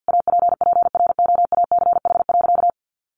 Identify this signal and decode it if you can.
Morse code, heard as a keyed tone, about 50 wpm, 700 Hz. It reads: APPROACH.
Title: Sentences from Top 1400 Words 50wpm